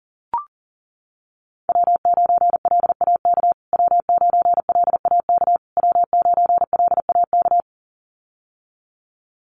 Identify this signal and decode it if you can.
Morse code, heard as a keyed tone, about 40 wpm, 700 Hz. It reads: W9LAK W9LAK W9LAK
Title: USA Call Signs Repeated 3x 40wpm